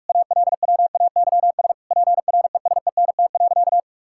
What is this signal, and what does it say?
Morse code, heard as a keyed tone, about 46 wpm, 700 Hz. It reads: ALWAYS PRESENT.
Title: Sentences from Top 1000 Words - Review Part 5 - 45wpm